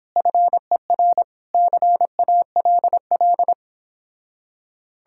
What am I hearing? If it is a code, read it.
Morse code, heard as a keyed tone, about 31 wpm, 700 Hz. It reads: FER CALL